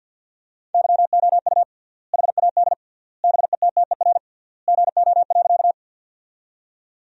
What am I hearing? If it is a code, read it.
Morse code, heard as a keyed tone, about 50 wpm, 700 Hz. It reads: YOU HAD BETTER GO.
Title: 3x Word-Spacing - Sentences from Top 400 Words - 50wpm